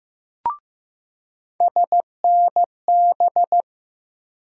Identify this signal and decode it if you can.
Morse code, heard as a keyed tone, about 15 wpm, 700 Hz. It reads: SNB
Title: USA Call Sign Postfixes - 3 Letters 15wpm